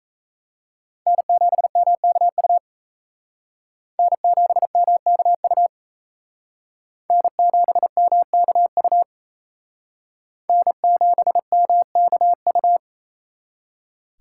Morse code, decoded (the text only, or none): N7MKU N7MKU N7MKU N7MKU